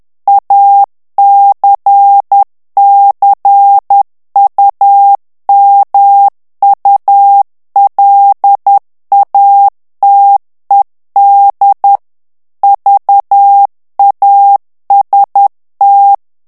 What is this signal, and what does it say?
Morse code, heard as a keyed tone, about 10 wpm, 800 Hz. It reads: ACCUMULATED VAST
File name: morse mp3 / ANGLAIS texte clair 2 vitesse 10 mots minute